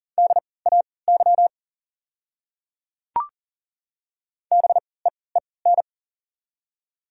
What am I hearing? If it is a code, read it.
Morse code, heard as a keyed tone, about 40 wpm, spaced as for 23 wpm, 690 Hz. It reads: DAY BEEN
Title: 3x Character Spacing - Top 200 Words - 40wpm